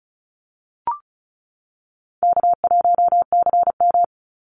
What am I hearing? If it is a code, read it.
Morse code, heard as a keyed tone, about 34 wpm, 700 Hz. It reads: K1CM
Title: NA SOTA Chaser Callsigns 35wpm